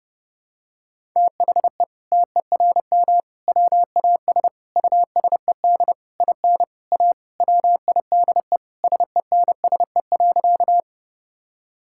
Morse code, heard as a keyed tone, about 32 wpm, 700 Hz. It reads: THE TERM WAS USED IN A WIDE SENSE.
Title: Sentences from Top 1000 Words - Review Part 6 - 30wpm